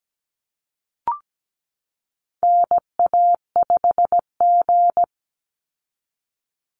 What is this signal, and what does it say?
Morse code, heard as a keyed tone, about 17 wpm, 700 Hz. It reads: NA5G
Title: Call Signs Repeated 3x - 17wpm